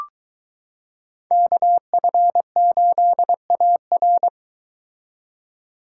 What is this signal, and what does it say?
Morse code, heard as a keyed tone, about 23 wpm, 700 Hz. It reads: KF8AR